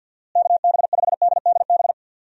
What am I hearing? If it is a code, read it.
Morse code, heard as a keyed tone, about 51 wpm, 700 Hz. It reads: KB5DDB
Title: USA Call Signs - 6 Characters - Repeated 2x - 50wpm